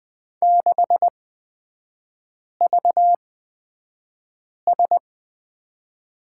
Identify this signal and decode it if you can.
Morse code, heard as a keyed tone, about 20 wpm, spaced as for 5 wpm, 690 Hz. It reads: 6VS
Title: 229 Lesson 37e - Letter x - Three Characters 20wpm